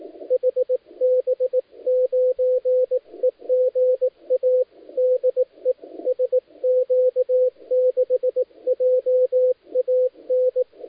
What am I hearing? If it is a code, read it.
Morse code, heard as a keyed tone, about 18 wpm, 500 Hz, with radio noise. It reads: HB9EGA DE SQ6JAN
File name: dlazaawansowanych / nagraniazpasma / titawaNETcontrol